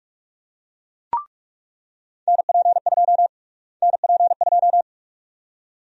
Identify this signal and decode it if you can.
Morse code, heard as a keyed tone, about 46 wpm, 700 Hz. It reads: NP2 NP2
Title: USA Call Sign Prefixes - Repeated 2x - 45wpm